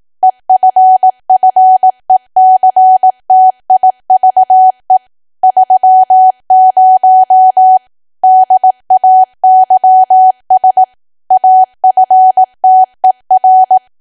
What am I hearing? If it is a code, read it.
Morse code, heard as a keyed tone, about 18 wpm, 750 Hz. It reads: EFFECTIVE 30 DAYS AFTER
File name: cwa / W1AW-Level2 / 18WPM